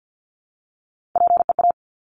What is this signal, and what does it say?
Morse code, heard as a keyed tone, about 50 wpm, 700 Hz. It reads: PEA
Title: Top Three Letter Combinations in Words 50wpm